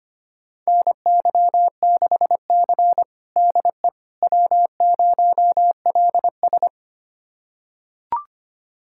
Morse code, heard as a keyed tone, about 25 wpm, 700 Hz. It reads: NY6C DE W0LS
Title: USA Callsign DE Callsign - 4 Characters - 25wpm